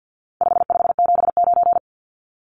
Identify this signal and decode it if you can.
Morse code, heard as a keyed tone, about 50 wpm, 700 Hz. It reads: 5579
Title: Numbers - 4-Digit 50wpm